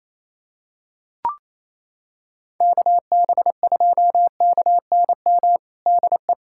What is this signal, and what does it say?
Morse code, heard as a keyed tone, about 28 wpm, 700 Hz. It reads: KB2KNM DE
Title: USA Callsign DE Callsign - 6 Characters - 28wpm